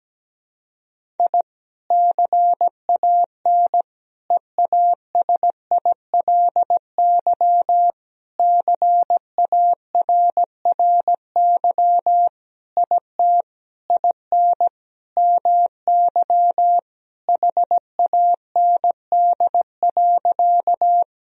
Morse code, heard as a keyed tone, about 17 wpm, 700 Hz. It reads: I CAN EASILY CARRY IT IN MY HAND.
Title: Sentences from Top 1000 Words - Review Part 1 - 17wpm